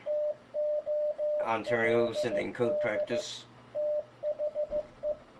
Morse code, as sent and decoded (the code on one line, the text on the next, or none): - --- .-.. -.. / - .... .
TOLD THE